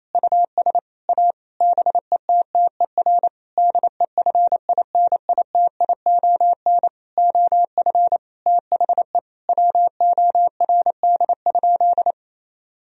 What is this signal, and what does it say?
Morse code, heard as a keyed tone, about 28 wpm, 700 Hz. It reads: US A BETTER DEFINITION OF THE WORD?
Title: Sentences from Top 1000 Words - Review Part 4 - 28wpm